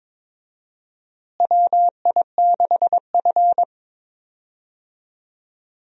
Morse code, heard as a keyed tone, about 22 wpm, 700 Hz. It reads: WI6F
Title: USA Call Signs - 4 Characters - Repeated 2x - 22wpm